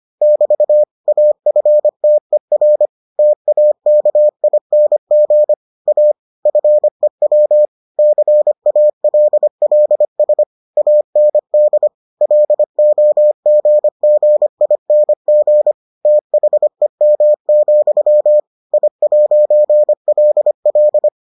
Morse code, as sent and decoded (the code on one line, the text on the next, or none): -...- / .- ..-. - . .-. / - .- -.- .. -. --. / .- / ..-. .-- / -.-. .- .-.. .-.. ... / .- -. -.. / .-.. --- --. --. .. -. --. / - .... -- --..-- / .. .----. .-.. .-..
<BT> AFTER TAKING A FEW CALLS AND LOGGING THEM, I'LL